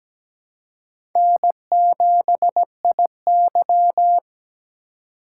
Morse code, heard as a keyed tone, about 17 wpm, 700 Hz. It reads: N7IY
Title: USA Call Signs - 4 Characters - 4x Character Spacing - 17wpm